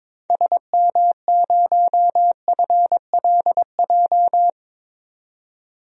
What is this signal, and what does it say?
Morse code, heard as a keyed tone, about 22 wpm, 700 Hz. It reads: SM0FLJ
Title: DXCC-100 Call Signs - 6 Characters - 22wpm